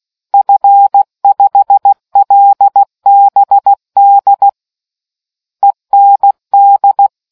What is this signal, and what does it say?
Morse code, heard as a keyed tone, about 16 wpm, 800 Hz. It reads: F5LBD END